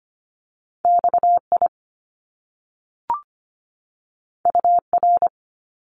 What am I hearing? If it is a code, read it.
Morse code, heard as a keyed tone, about 25 wpm, 700 Hz. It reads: XI UR
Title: Top Two Letter Combinations in Words 25wpm